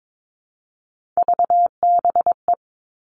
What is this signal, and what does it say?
Morse code, heard as a keyed tone, about 22 wpm, 700 Hz. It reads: VBE
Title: Three Challenging Letter-Number Combinations 22wpm